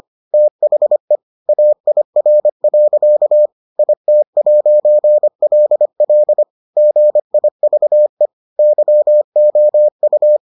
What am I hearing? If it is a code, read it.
Morse code, heard as a keyed tone, about 25 wpm, 600 Hz. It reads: THE AIR. IT'LL GIVE YOU